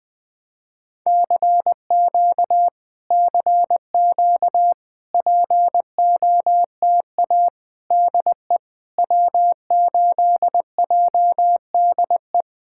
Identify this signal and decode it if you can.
Morse code, heard as a keyed tone, about 20 wpm, 700 Hz. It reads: CQ CQ POTA DE W8JDE
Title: POTA QSO Practice - 20wpm